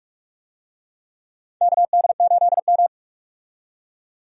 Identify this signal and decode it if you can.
Morse code, heard as a keyed tone, about 46 wpm, 700 Hz. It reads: KD8M
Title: USA Call Signs - 4 Characters - 45wpm